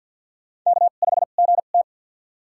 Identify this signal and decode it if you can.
Morse code, heard as a keyed tone, about 46 wpm, spaced as for 35 wpm, 700 Hz. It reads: K5GT
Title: USA Call Signs - 4 Characters - 2x Character Spacing - 50wpm